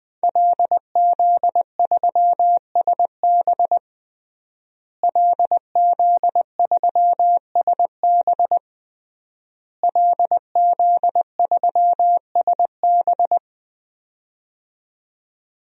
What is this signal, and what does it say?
Morse code, heard as a keyed tone, about 21 wpm, 700 Hz. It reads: LZ3SB LZ3SB LZ3SB
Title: DXCC-100 Call Signs - 5 Characters - Repeated 3x - 20wpm